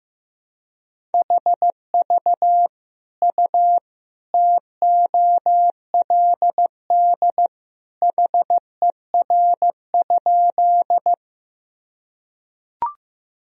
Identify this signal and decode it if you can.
Morse code, heard as a keyed tone, about 15 wpm, 700 Hz. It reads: HV U TOLD HER?
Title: Sentences from Top 400 Words and Abbreviations 15wpm